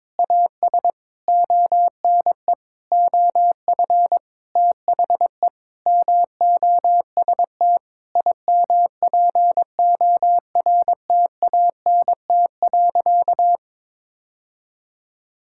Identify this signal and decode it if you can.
Morse code, heard as a keyed tone, about 22 wpm, 700 Hz. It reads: AS ONE OF THE MOST IMPORTANT.
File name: Sentences with 9-Word Phrases - 22wpm